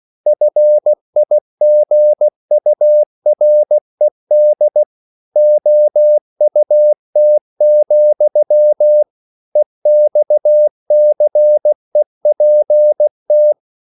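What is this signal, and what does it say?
Morse code, heard as a keyed tone, about 16 wpm, 600 Hz. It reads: FIGURED OUT, EXCEPT
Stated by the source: English quotations sent in CW as practice at 16 words per minute.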